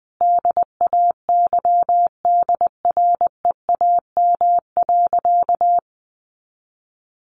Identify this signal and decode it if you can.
Morse code, heard as a keyed tone, about 20 wpm, 700 Hz. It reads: DAYDREAM.
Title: Easy Phrases 20wpm